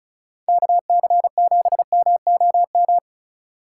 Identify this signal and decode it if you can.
Morse code, heard as a keyed tone, about 33 wpm, 700 Hz. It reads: KC7MOM
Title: Speed Racing - USA 2x3 Call Signs 35wpm